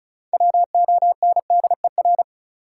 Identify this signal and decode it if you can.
Morse code, heard as a keyed tone, about 34 wpm, 700 Hz. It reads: WONDER
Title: Sentences from Top 500 Words - New Word Drill - 35wpm